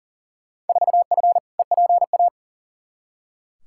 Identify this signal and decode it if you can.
Morse code, heard as a keyed tone, about 40 wpm, 700 Hz. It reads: 4F EPA
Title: Field Day Exchanges Encore - 40wpm